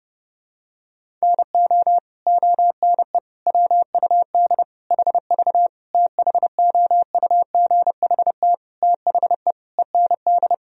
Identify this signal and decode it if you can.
Morse code, heard as a keyed tone, about 30 wpm, 700 Hz. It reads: NO ONE WUD HV THOUGHT THE END
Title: Sentences from Top 300 Words and Abbreviations 30wpm